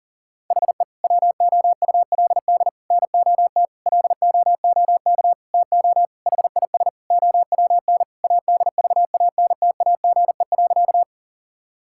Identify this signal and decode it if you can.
Morse code, heard as a keyed tone, about 40 wpm, 700 Hz. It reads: HE WOULD NOT LOOK TO HIS OWN ADVANTAGE.
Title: Sentences from Top 1000 Words - Review Part 6 - 40wpm